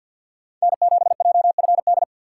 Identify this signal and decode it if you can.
Morse code, heard as a keyed tone, about 53 wpm, 700 Hz. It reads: N7JFD
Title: USA Call Signs - 5 Characters - 8x Character Spacing - 50wpm